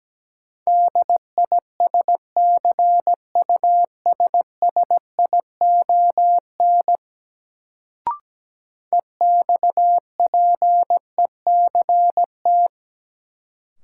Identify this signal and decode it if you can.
Morse code, heard as a keyed tone, about 17 wpm, 700 Hz. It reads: DISCUSSION EXPECT
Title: Sentences from Top 1100 Words - New Word Drill - 17wpm